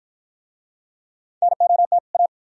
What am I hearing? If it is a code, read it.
Morse code, heard as a keyed tone, about 53 wpm, 700 Hz. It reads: NOT A